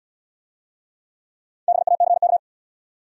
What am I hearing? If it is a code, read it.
Morse code, heard as a keyed tone, about 72 wpm, 700 Hz. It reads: BACK